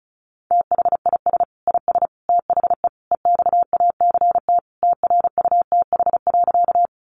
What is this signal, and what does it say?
Morse code, heard as a keyed tone, about 34 wpm, 700 Hz. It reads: THIS IS THE EXACT TRUTH.
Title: Sentences from Top 1300 Words 35wpm